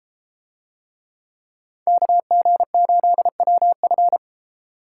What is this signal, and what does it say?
Morse code, heard as a keyed tone, about 33 wpm, 700 Hz. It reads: KG8WF